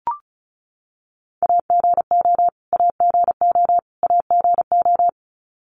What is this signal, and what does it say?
Morse code, heard as a keyed tone, about 34 wpm, 700 Hz. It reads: AGO AGO AGO
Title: Top 500 Words Repeated 35wpm